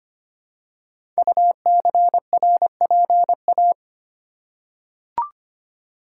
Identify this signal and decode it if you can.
Morse code, heard as a keyed tone, about 25 wpm, 700 Hz. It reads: UCRPA
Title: Code Groups - 5 Letter:Numbers - 25wpm